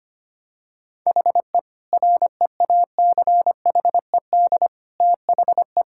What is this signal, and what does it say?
Morse code, heard as a keyed tone, about 25 wpm, 700 Hz. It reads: HE REACHED THE
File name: practice / 25wpm / Sentences with 3-Word Phrases - 25wpm